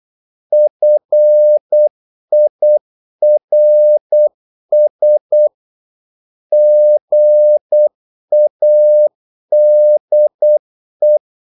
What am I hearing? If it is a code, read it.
Morse code, heard as a keyed tone, about 8 wpm, 600 Hz. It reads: FIRS GADE
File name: Korte ord DK 8wpm